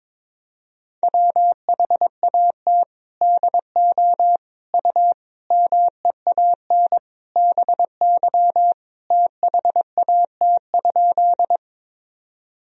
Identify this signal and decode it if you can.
Morse code, heard as a keyed tone, about 22 wpm, 700 Hz. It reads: WHAT DO U MEAN BY THAT?
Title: Sentences from Top 200 Words and Abbreviations 22wpm